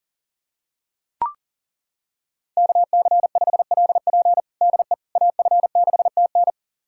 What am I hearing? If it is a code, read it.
Morse code, heard as a keyed tone, about 37 wpm, 700 Hz. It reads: KC5LP DE AF6TN
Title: USA Callsign DE Callsign - 5 Characters - 40wpm